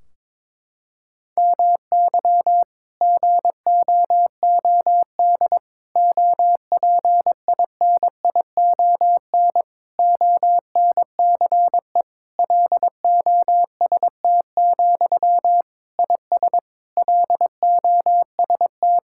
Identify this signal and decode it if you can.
Morse code, heard as a keyed tone, about 22 wpm, 700 Hz. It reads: MY GOOD OPINION ONCE LOST, IS LOST
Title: Sentences from Top 1600 Words 22wpm